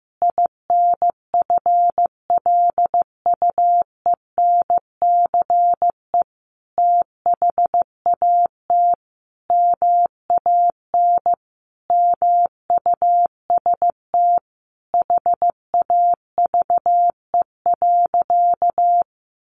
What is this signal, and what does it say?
Morse code, heard as a keyed tone, about 15 wpm, 700 Hz. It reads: INFLUENCE THAT MAN MUST HAVE.
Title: Sentences from Top 1100 Words 15wpm